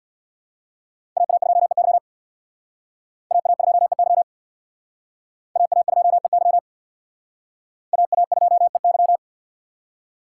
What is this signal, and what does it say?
Morse code, heard as a keyed tone, about 61 wpm, 700 Hz. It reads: AA2EY AA2EY AA2EY AA2EY
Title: Speed Racing - USA 2x2 Call Signs 50wpm